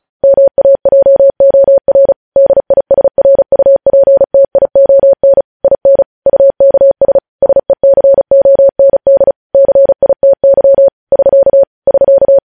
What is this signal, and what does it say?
Morse code, heard as a keyed tone, about 36 wpm, 550 Hz. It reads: MAJOR DISRUPTION IN UKS SECOND CITY <SK> <SK>
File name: podcast / News headlines-1x-35x35